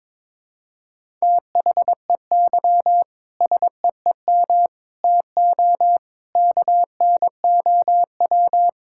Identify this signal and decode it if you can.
Morse code, heard as a keyed tone, about 22 wpm, 700 Hz. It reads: THEY SEEM TO KNOW
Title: Sentences from Top 1000 Words - Review Part 6 - 22wpm